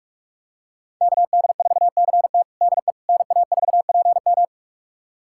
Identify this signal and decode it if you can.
Morse code, heard as a keyed tone, about 46 wpm, 700 Hz. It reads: KD4CT DE NA4PM
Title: USA Callsign DE Callsign - 5 Characters - 45wpm